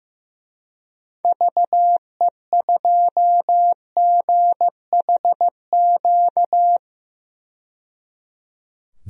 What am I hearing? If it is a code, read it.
Morse code, heard as a keyed tone, about 15 wpm, 700 Hz. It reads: VE2GHQ
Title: DXCC-100 Call Signs - 6 Characters - 15wpm